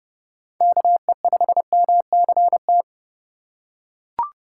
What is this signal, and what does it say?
Morse code, heard as a keyed tone, about 32 wpm, 700 Hz. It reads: KE5MCT
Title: USA Call Signs - 6 Characters - 2x Character Spacing - 30wpm